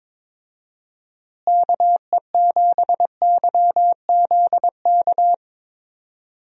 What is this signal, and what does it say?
Morse code, heard as a keyed tone, about 22 wpm, 700 Hz. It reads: KE7YZK
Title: USA Call Signs - 6 Characters - Repeated 2x - 22wpm